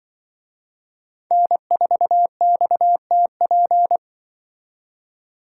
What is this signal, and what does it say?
Morse code, heard as a keyed tone, about 25 wpm, 700 Hz. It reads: N4XTP